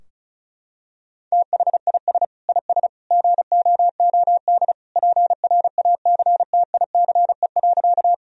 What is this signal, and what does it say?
Morse code, heard as a keyed tone, about 36 wpm, 700 Hz. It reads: THIS IS GOOD PRACTICE.
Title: Sentences from Top 1000 Words - Review Part 4 - 35wpm